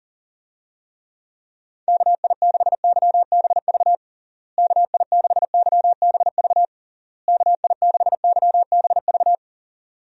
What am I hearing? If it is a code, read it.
Morse code, heard as a keyed tone, about 37 wpm, 700 Hz. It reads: KI6YBV KI6YBV KI6YBV
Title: USA Call Signs - 6 Characters - Repeated 3x - 40wpm